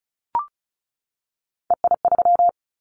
Morse code, heard as a keyed tone, about 36 wpm, 700 Hz. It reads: EI3
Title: Callsign Prefixes for Top 50 DXCC - 35wpm